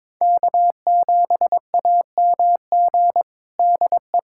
Morse code, heard as a keyed tone, about 22 wpm, 700 Hz. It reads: K7AMG DE
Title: USA Callsign DE Callsign 22wpm